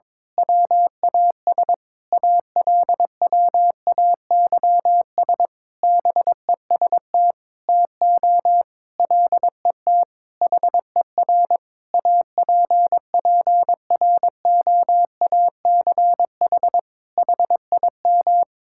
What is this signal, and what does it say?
Morse code, heard as a keyed tone, about 22 wpm, 700 Hz. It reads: WAS ALWAYS BEST TO LET HER APPROACH HIM
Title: Sentences from Top 1000 Words - Review Part 6 - 22wpm